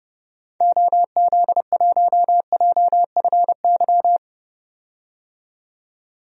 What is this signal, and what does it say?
Morse code, heard as a keyed tone, about 30 wpm, 700 Hz. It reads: OZ1JFY
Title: DXCC-100 Call Signs - 6 Characters - 30wpm